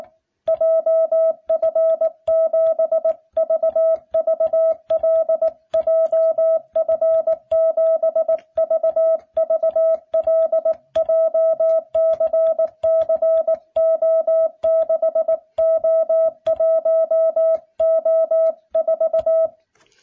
Morse code, heard as a keyed tone, about 18 wpm, 660 Hz. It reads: JF7VVL JF7VVL JCCO6O1O4